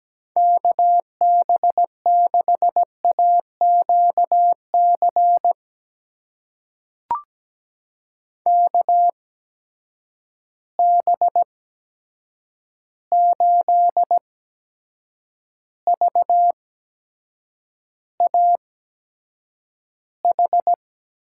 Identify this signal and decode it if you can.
Morse code, heard as a keyed tone, about 17 wpm, spaced as for 7 wpm, 700 Hz. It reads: KB6AQC KB8VAH
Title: USA Call Signs - 6 Characters - 8x Character Spacing - 17wpm